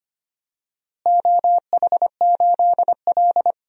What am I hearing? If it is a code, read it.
Morse code, heard as a keyed tone, about 25 wpm, 700 Hz. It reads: OH8L
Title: DXCC-100 Call Signs - 4 Characters - Repeated 3x - 25wpm